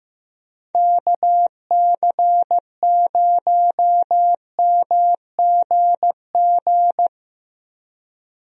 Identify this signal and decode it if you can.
Morse code, heard as a keyed tone, about 15 wpm, 700 Hz. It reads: KC0MGG
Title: USA Call Signs - 6 Characters - Repeated 2x - 15wpm